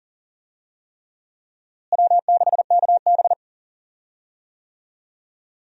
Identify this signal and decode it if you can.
Morse code, heard as a keyed tone, about 40 wpm, 700 Hz. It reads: W6KB
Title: USA Call Signs - 4 Characters - Repeated 3x - 40wpm